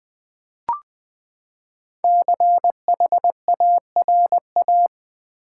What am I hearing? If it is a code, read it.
Morse code, heard as a keyed tone, about 20 wpm, 700 Hz. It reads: CHARA
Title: Build A Word - 5 Letters At A Time - 12-Letter Words - 20wpm